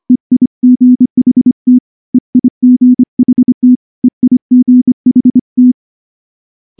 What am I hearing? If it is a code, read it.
Morse code, heard as a keyed tone, about 24 wpm, 250 Hz. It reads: EIGHT EIGHT EIGHT
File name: morse / morse-training-25WPM